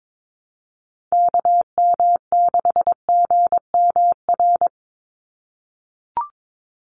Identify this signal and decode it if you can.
Morse code, heard as a keyed tone, about 22 wpm, 700 Hz. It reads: KM6GMR
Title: USA Call Signs 22wpm